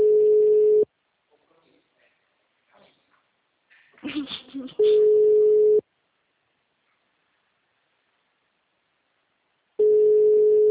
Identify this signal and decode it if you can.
Ringback tone (Europe, 425 Hz), with speech.